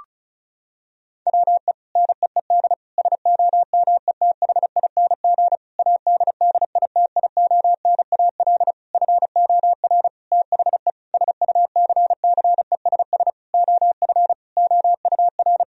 Morse code, heard as a keyed tone, about 36 wpm, 700 Hz. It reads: WE NEED SOMETHING ADDITIONAL FOR THE SUCCESS OF OUR
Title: Sentences from Top 1000 Words - Review Part 4 - 35wpm